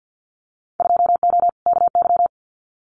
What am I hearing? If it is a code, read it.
Morse code, heard as a keyed tone, about 50 wpm, 700 Hz. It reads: 2O KY
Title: Winter Field Day Exchanges 50wpm